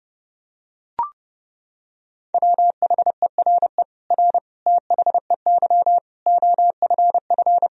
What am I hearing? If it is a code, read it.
Morse code, heard as a keyed tone, about 30 wpm, 700 Hz. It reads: WHERE R THEY OFF
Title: Sentences from Top 200 Words and Abbreviations 30wpm